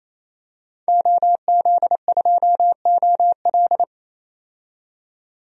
Morse code, heard as a keyed tone, about 28 wpm, 700 Hz. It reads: OZ2OL
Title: DXCC-100 Call Signs - 5 Characters - 28wpm